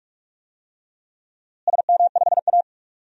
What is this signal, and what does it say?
Morse code, heard as a keyed tone, about 41 wpm, 700 Hz. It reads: SM5A
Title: DXCC-100 Call Signs - 4 Characters - Repeated 3x - 45wpm